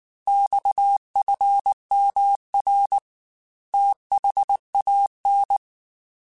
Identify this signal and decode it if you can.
Morse code, heard as a keyed tone, about 19 wpm, 800 Hz. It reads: XFMR THAN
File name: Technical Subjects / Technical Subjects-mp3s / Sw Pwr Sup @18